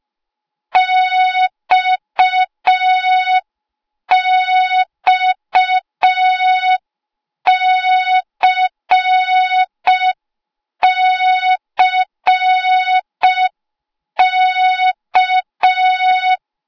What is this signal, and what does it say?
Morse code, heard as a keyed tone, about 5 wpm, 3700 Hz. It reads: XXCCK